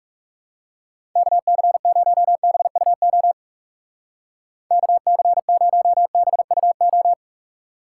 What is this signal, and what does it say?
Morse code, heard as a keyed tone, about 39 wpm, 700 Hz. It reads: KC0BUO KC0BUO